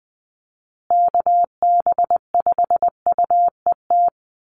Morse code, heard as a keyed tone, about 20 wpm, 700 Hz. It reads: KB5UET
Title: USA Call Signs Repeated 3x 20wpm